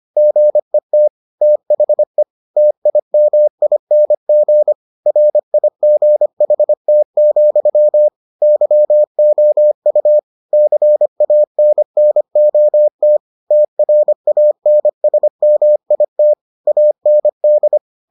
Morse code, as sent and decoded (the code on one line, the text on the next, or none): --. - / - .... / - .. -- .. -. --. / .-. .. --. .... - --..-- / -.-- --- ..- / -.-. .- -. -. --- - / - .-. .- -. ... -- .. - / .- -. -..
GET THE TIMING RIGHT, YOU CANNOT TRANSMIT AND